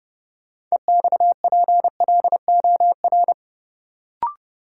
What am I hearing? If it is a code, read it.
Morse code, heard as a keyed tone, about 30 wpm, 700 Hz. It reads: EXPLOR
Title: Sentences from Top 1100 Words - New Word Drill - 30wpm